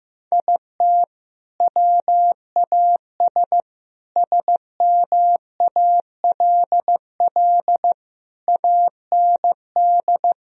Morse code, heard as a keyed tone, about 15 wpm, 700 Hz. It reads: IT WAS SMALL AND
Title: Sentences from Top 1000 Words - Review Part 1 - 15wpm